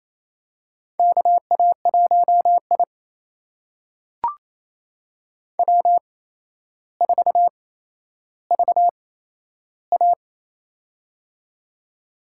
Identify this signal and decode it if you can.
Morse code, heard as a keyed tone, about 28 wpm, spaced as for 12 wpm, 700 Hz. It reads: KA1I W4VA
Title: USA Call Signs - 4 Characters - 8x Character Spacing - 28wpm